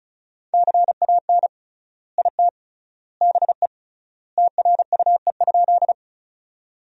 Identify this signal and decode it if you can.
Morse code, heard as a keyed tone, about 36 wpm, 700 Hz. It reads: CAN IT BE TRUE?
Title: 3x Word-Spacing - Sentences from Top 400 Words - 35wpm